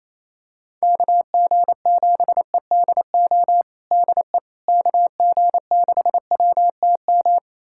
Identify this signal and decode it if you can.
Morse code, heard as a keyed tone, about 28 wpm, 700 Hz. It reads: KG7EDO DE KG6WTM
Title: USA Callsign DE Callsign 28wpm